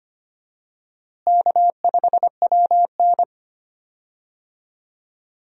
Morse code, heard as a keyed tone, about 25 wpm, 700 Hz. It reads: K5WN
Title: USA Call Signs - 4 Characters - Repeated 3x - 25wpm